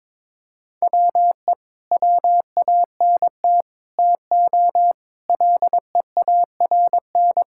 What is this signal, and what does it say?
Morse code, heard as a keyed tone, about 22 wpm, 700 Hz. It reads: WE WANT TO LEARN
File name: practice / 22wpm / Sentences with 3-Word Phrases - 22wpm